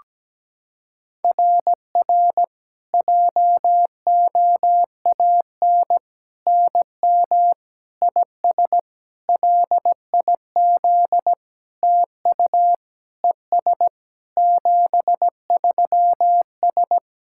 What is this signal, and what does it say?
Morse code, heard as a keyed tone, about 17 wpm, 700 Hz. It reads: RR JOAN NM IS LIZ TU ES 73S
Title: SOTA QSO Practice 17wpm